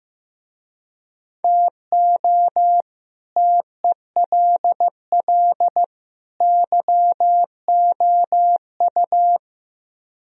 Morse code, heard as a keyed tone, about 16 wpm, 700 Hz. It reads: TO TELL YOU
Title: Speed-Racing - Sets of 3 Words 15wpm